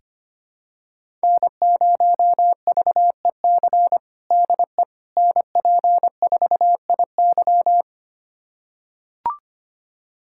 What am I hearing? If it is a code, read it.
Morse code, heard as a keyed tone, about 25 wpm, 700 Hz. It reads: N0VEC DE NP4IY
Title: USA Callsign DE Callsign - 5 Characters - 25wpm